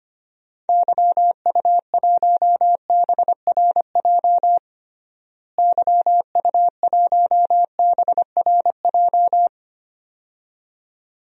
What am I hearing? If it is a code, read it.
Morse code, heard as a keyed tone, about 25 wpm, 700 Hz. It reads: YU1BRJ YU1BRJ
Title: DXCC-100 Call Signs - 6 Characters - Repeated 2x - 25wpm